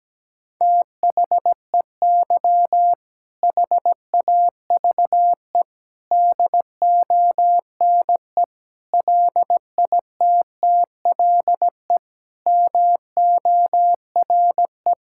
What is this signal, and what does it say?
Morse code, heard as a keyed tone, about 17 wpm, 700 Hz. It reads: THEY HAVE DONE LITTLE MORE
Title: Sentences from Top 1000 Words - Review Part 5 - 17wpm